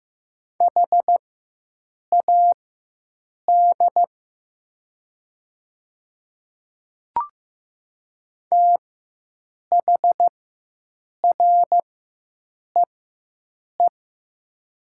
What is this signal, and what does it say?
Morse code, heard as a keyed tone, about 15 wpm, spaced as for 7 wpm, 700 Hz. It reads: HAD THREE
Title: 4x Character Spacing - Top 200 Words - 15wpm